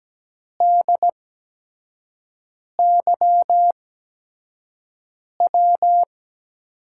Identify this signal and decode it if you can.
Morse code, heard as a keyed tone, about 17 wpm, spaced as for 5 wpm, 700 Hz. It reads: DYW
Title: USA Call Signs - 8x Character Spacing - 17wpm